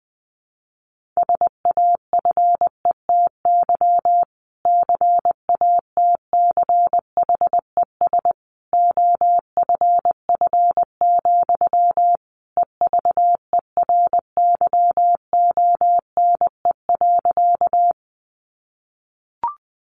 Morse code, heard as a keyed tone, about 20 wpm, 700 Hz. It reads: SAFETY CATCHES OFF, EVERYONE.
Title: A Sound Of Thunder 20wpm